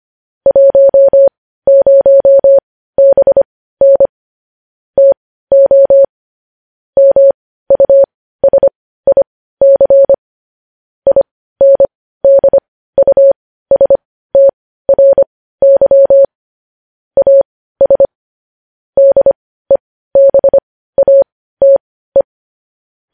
Morse code, heard as a keyed tone, about 25 wpm, spaced as for 15 wpm, 550 Hz. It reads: 10BN TO MUSIC INDUSTRY AS DEBATE
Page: news-headlines-in-morse-code-at-15-wpm